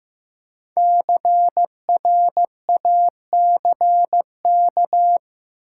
Morse code, heard as a keyed tone, about 15 wpm, 700 Hz. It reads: CRACK
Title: Five Letter Words - 15wpm